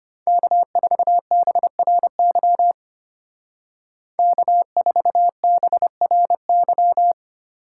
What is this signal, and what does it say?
Morse code, heard as a keyed tone, about 33 wpm, 700 Hz. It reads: K4BRY K4BRY